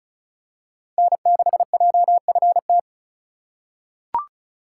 Morse code, heard as a keyed tone, about 36 wpm, 700 Hz. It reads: N6JFT
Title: USA Call Signs - 4x Character Spacing - 35wpm